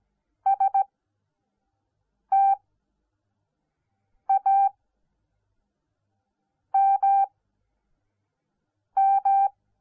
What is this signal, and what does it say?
Morse code, heard as a keyed tone, about 17 wpm, spaced as for 4 wpm, 780 Hz. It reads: STAMM